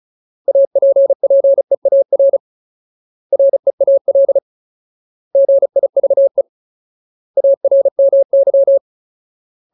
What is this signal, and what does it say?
Morse code, heard as a keyed tone, about 34 wpm, 550 Hz. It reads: APPEAR REAL GIVE ARMY
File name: Ultimate Word List / 35WPM Ultimate